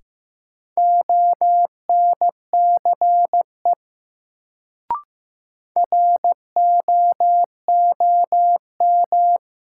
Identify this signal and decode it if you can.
Morse code, heard as a keyed tone, about 15 wpm, 700 Hz. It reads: ONCE ROOM
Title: Sentences from Top 300 Words - New Word Drill - 15wpm